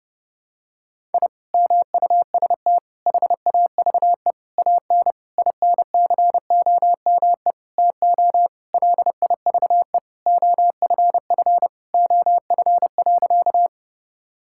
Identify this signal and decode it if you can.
Morse code, heard as a keyed tone, about 32 wpm, 700 Hz. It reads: I MUST HAVE AN INCOME TO LIVE OFF OF.
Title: Sentences from Top 1000 Words - Review Part 5 - 30wpm